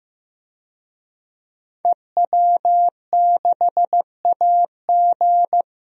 Morse code, heard as a keyed tone, about 15 wpm, 700 Hz. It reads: EW6AG
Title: DXCC-100 Call Signs - 5 Characters - Repeated 3x - 15wpm